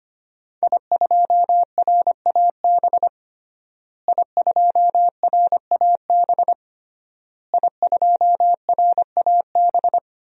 Morse code, heard as a keyed tone, about 25 wpm, 700 Hz. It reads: I2RAB I2RAB I2RAB
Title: DXCC-100 Call Signs - 5 Characters - Repeated 3x - 25wpm